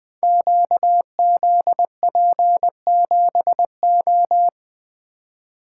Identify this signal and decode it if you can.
Morse code, heard as a keyed tone, about 20 wpm, 700 Hz. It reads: QZP7O
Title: Code Groups - 5 Letter:Numbers - 20wpm